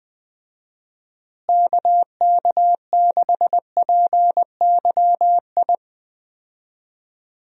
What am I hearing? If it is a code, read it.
Morse code, heard as a keyed tone, about 20 wpm, 700 Hz. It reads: KK6PYI
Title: USA Call Signs - 6 Characters - Repeated 3x - 20wpm